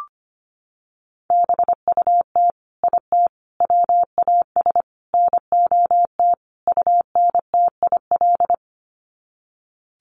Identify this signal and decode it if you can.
Morse code, heard as a keyed tone, about 25 wpm, 700 Hz. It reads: BUT IT WAS NOT UNTIL
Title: Sets of 5 Words Encore2 25wpm